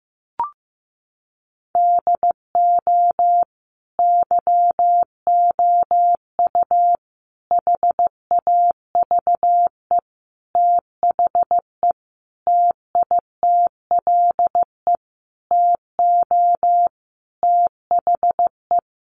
Morse code, heard as a keyed tone, about 15 wpm, 700 Hz. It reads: DO YOU HAVE THE TITLE TO THE
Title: Sentences from Top 1200 Words 15wpm